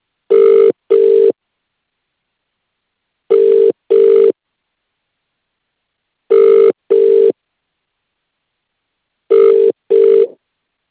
Ringback tone (UK), 400 + 450 Hz.